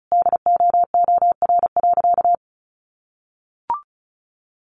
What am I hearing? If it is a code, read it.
Morse code, heard as a keyed tone, about 34 wpm, 700 Hz. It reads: DOOR.
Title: Sentences from Top 1400 Words 35wpm